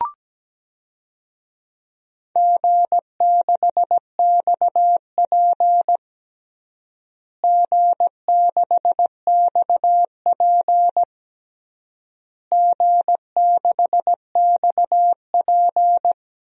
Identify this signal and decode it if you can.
Morse code, heard as a keyed tone, about 17 wpm, 700 Hz. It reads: G6XP G6XP G6XP
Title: DXCC-100 Call Signs - 4 Characters - Repeated 3x - 17wpm